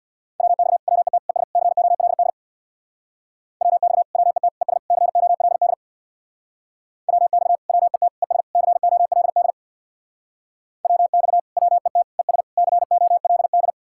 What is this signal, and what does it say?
Morse code, heard as a keyed tone, about 63 wpm, 700 Hz. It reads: WX WET ES COLD WX WET ES COLD WX WET ES COLD WX WET ES COLD